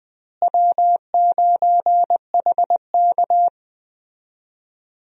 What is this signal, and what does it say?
Morse code, heard as a keyed tone, about 20 wpm, 700 Hz. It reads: W9HK
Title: USA Call Signs - 4 Characters - Repeated 2x - 20wpm